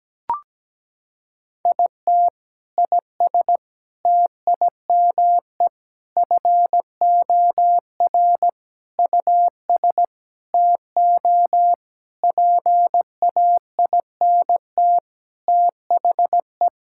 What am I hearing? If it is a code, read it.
Morse code, heard as a keyed tone, about 17 wpm, 700 Hz. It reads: IT IS TIME FOR US TO PAINT THE
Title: Sentences from Top 1000 Words - Review Part 6 - 17wpm